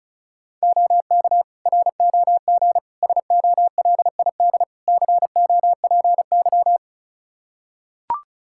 Morse code, heard as a keyed tone, about 34 wpm, 700 Hz. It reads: OK ROG SOLID COPY
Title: Practice QSO - Exchange 3 - More Info - 35wpm